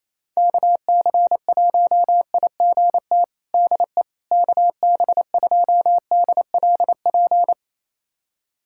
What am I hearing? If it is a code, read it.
Morse code, heard as a keyed tone, about 28 wpm, 700 Hz. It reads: KC1IGT DE KB2DLP
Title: USA Callsign DE Callsign - 6 Characters - 28wpm